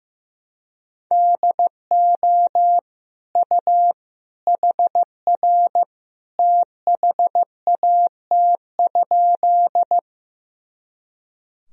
Morse code, heard as a keyed tone, about 15 wpm, 700 Hz. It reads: DO U HR THAT?
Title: Sentences from Top 300 Words and Abbreviations 15wpm